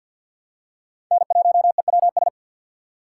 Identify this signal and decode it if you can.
Morse code, heard as a keyed tone, about 53 wpm, 690 Hz. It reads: N1EWS
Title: USA Call Signs - 5 Characters - 3x Character Spacing - 50wpm